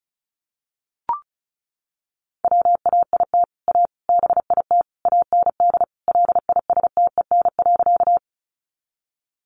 Morse code, heard as a keyed tone, about 34 wpm, 700 Hz. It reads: WAIT A BIT AND LISTEN.
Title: Sentences from Top 900 Words 35wpm